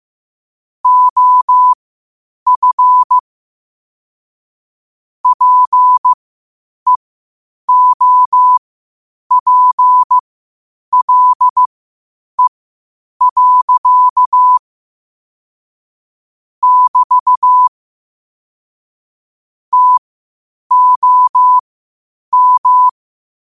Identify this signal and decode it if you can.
Morse code, heard as a keyed tone, about 15 wpm, spaced as for 8 wpm, 1000 Hz. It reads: OF PEOPLE. <BT> TOM
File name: qotdfiles / 8wpm